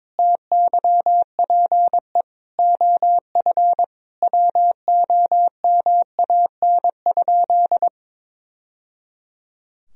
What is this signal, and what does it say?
Morse code, heard as a keyed tone, about 22 wpm, 700 Hz. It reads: TYPE OF WOMAN?
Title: Sentences from Top 1000 Words - Review Part 3 - 22wpm